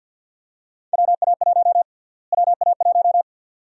Morse code, heard as a keyed tone, about 53 wpm, 700 Hz. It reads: WA1 WA1
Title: USA Call Sign Prefixes - Repeated 2x - 50wpm